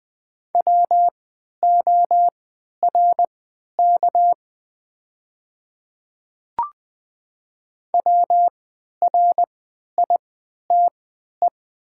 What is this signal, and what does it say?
Morse code, heard as a keyed tone, about 20 wpm, spaced as for 11 wpm, 700 Hz. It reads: WORK WRITE
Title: 3x Character Spacing - Top 200 Words - 20wpm